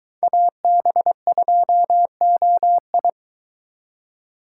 Morse code, heard as a keyed tone, about 23 wpm, 700 Hz. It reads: AB2OI